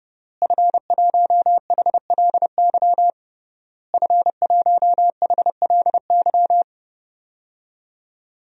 Morse code, heard as a keyed tone, about 30 wpm, 700 Hz. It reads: F1HLY F1HLY
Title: DXCC-100 Call Signs - 5 Characters - Repeated 3x - 30wpm